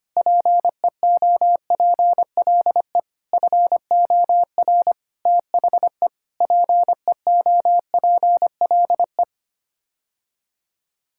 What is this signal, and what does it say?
Morse code, heard as a keyed tone, about 25 wpm, 700 Hz. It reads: PEOPLE FOR THE PEOPLE
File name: Sets of 9 Words - 25wpm